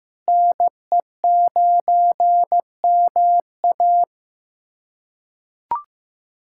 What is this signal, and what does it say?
Morse code, heard as a keyed tone, about 15 wpm, 700 Hz. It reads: NE9MA
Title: USA Callsign DE Callsign - 5 Characters - 15wpm